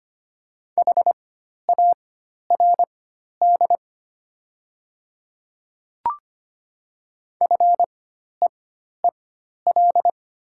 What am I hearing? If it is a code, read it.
Morse code, heard as a keyed tone, about 25 wpm, spaced as for 12 wpm, 700 Hz. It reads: HARD FEEL
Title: 4x Character Spacing - Top 500 Words - 25wpm